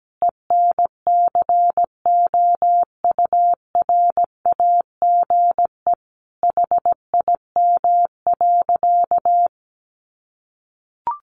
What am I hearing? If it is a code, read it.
Morse code, heard as a keyed tone, about 17 wpm, 700 Hz. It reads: ENCOURAGE HIM.
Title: Sentences from Top 1400 Words 17wpm